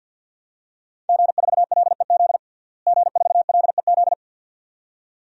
Morse code, heard as a keyed tone, about 52 wpm, 700 Hz. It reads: G4LEZ G4LEZ
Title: DXCC-100 Call Signs - 5 Characters - Repeated 2x - 50wpm